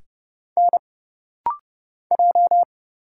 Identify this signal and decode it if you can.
Morse code, heard as a keyed tone, about 29 wpm, 700 Hz. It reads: N J
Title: Single Letters - Rapid-Fire 30wpm